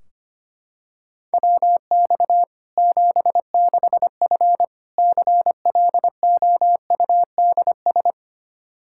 Morse code, heard as a keyed tone, about 25 wpm, 700 Hz. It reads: WX 76F CLOUDS
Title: Practice QSO - Exchange 2 - Info - 25wpm